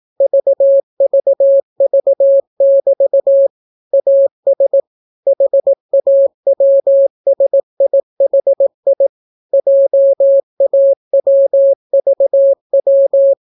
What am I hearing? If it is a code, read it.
Morse code, heard as a keyed tone, about 18 wpm, 550 Hz. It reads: VVV<BT> AS HAWSIHI JAWVW